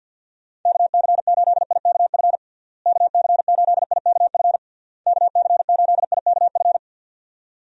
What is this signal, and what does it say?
Morse code, heard as a keyed tone, about 46 wpm, 700 Hz. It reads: KC8IKF KC8IKF KC8IKF
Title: USA Call Signs - 6 Characters - Repeated 3x - 50wpm